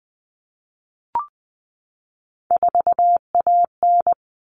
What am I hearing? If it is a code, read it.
Morse code, heard as a keyed tone, about 20 wpm, 700 Hz. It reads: 4AN
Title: All Cut Numbers - 3-Digit 20wpm